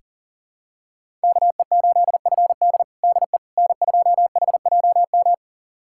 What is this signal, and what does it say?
Morse code, heard as a keyed tone, about 40 wpm, 700 Hz. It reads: KE8FD DE N2HJM
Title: USA Callsign DE Callsign - 5 Characters - 40wpm